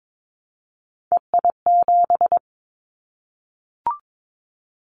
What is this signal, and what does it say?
Morse code, heard as a keyed tone, about 22 wpm, 700 Hz. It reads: EI7
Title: Callsign Prefixes for Top 50 DXCC - 22wpm